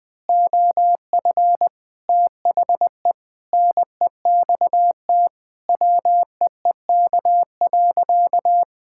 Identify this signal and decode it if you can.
Morse code, heard as a keyed tone, about 20 wpm, 700 Hz. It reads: OF THE NEXT WEEK.
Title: Sentences with 6-Word Phrases - 20wpm